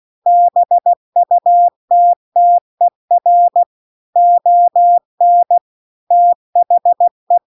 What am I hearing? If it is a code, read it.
Morse code, heard as a keyed tone, about 16 wpm, 700 Hz. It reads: BUTTER ON THE
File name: CW-The-Big-Twoheaded-River-2-16WPM